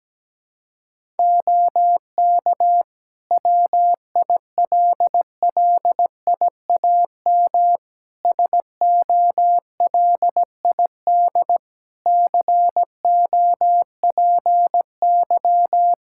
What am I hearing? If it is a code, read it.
Morse code, heard as a keyed tone, about 17 wpm, 700 Hz. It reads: OK WILLIAM SOLID COPY
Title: Practice QSO - Exchange 3 - More Info - 17wpm